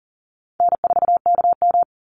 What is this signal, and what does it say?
Morse code, heard as a keyed tone, about 40 wpm, 700 Hz. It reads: N4KM
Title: Call Signs Repeated 2x - 40wpm